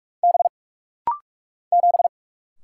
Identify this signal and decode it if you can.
Morse code, heard as a keyed tone, about 45 wpm, 700 Hz. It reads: B 7
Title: Single Letter-Number - Rapid-Fire 45wpm